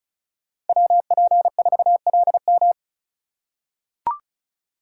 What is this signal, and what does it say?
Morse code, heard as a keyed tone, about 36 wpm, 700 Hz. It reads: WP4LM
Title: USA Call Signs - 4x Character Spacing - 35wpm